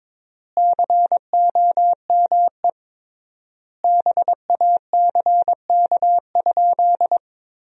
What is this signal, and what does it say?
Morse code, heard as a keyed tone, about 22 wpm, 700 Hz. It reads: COME BACK?
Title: 3x Word-Spacing - Sentences from Top 200 Words - 22wpm